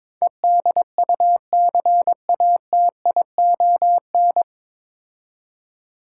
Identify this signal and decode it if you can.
Morse code, heard as a keyed tone, about 24 wpm, 700 Hz. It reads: EDUCATION
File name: Easy Words - Speed-Racing with Hints - 22wpm